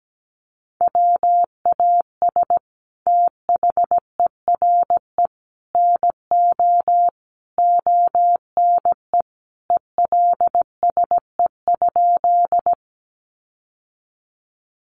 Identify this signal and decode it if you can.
Morse code, heard as a keyed tone, about 17 wpm, 700 Hz. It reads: WAS THERE NO ONE ELSE?
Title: Sentences from Top 800 Words 17wpm